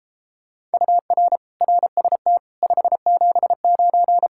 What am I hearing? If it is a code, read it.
Morse code, heard as a keyed tone, about 34 wpm, 700 Hz. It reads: UR RST 579